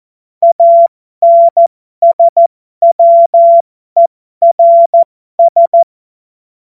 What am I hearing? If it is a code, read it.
Morse code, heard as a keyed tone, about 14 wpm, spaced as for 12 wpm, 680 Hz. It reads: ANSWERS
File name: Morsekurs / Witze